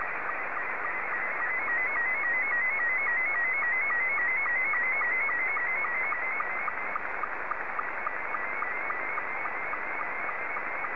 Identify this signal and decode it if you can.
SSTV (slow-scan television), Scottie 2.